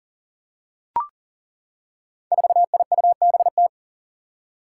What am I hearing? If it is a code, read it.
Morse code, heard as a keyed tone, about 40 wpm, 700 Hz. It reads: 4IUBT
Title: Five Challenging Letter-Number Combinations 40wpm